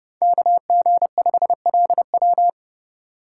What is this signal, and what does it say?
Morse code, heard as a keyed tone, about 30 wpm, 700 Hz. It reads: KG5LW
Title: USA Call Signs - 5 Characters - 4x Character Spacing - 30wpm